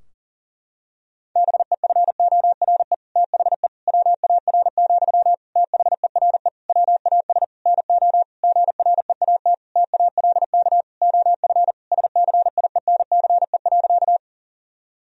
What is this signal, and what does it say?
Morse code, heard as a keyed tone, about 40 wpm, 700 Hz. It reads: BEFORE THE WAR, THERE WAS NO GREAT TALK OF SCIENCE.
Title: Sentences from Top 1000 Words - Review Part 2 - 40wpm